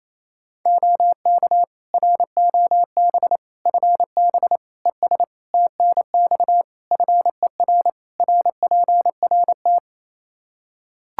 Morse code, heard as a keyed tone, about 28 wpm, 700 Hz. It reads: OK ROB FB ES TNX FER RPRT
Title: Practice QSO - Exchange 2 - Info - 28wpm